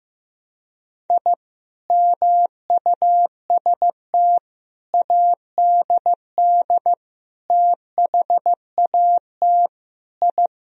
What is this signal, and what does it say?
Morse code, heard as a keyed tone, about 15 wpm, 700 Hz. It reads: I MUST ADD THAT I
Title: Sentences from Top 1000 Words - Review Part 1 - 15wpm